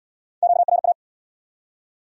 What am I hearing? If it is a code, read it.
Morse code, heard as a keyed tone, about 64 wpm, 700 Hz. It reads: QRA